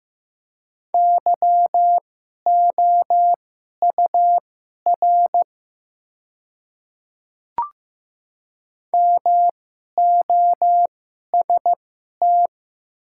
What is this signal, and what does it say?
Morse code, heard as a keyed tone, about 15 wpm, spaced as for 11 wpm, 700 Hz. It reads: YOUR MOST
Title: 2x Character Spacing - Top 100 Words - 15wpm